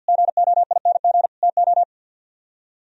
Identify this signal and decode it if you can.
Morse code, heard as a keyed tone, about 60 wpm, 700 Hz. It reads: GOING TO
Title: Speed-Racing - Sets of 5 Words 50wpm